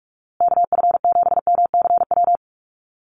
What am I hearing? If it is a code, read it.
Morse code, heard as a keyed tone, about 44 wpm, 700 Hz. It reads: KF7MCW
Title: USA Call Signs Repeated 2x 45wpm